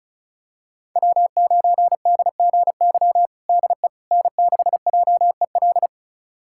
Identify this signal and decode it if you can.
Morse code, heard as a keyed tone, about 36 wpm, 700 Hz. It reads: W9DGY DE N6JEL
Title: USA Callsign DE Callsign - 5 Characters - 35wpm